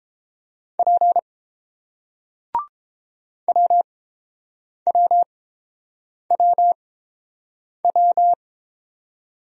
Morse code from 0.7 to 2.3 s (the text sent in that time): P